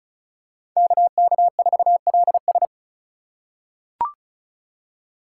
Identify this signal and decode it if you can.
Morse code, heard as a keyed tone, about 35 wpm, 700 Hz. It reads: KK4LS
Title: USA Call Signs - 5 Characters - 2x Character Spacing - 35wpm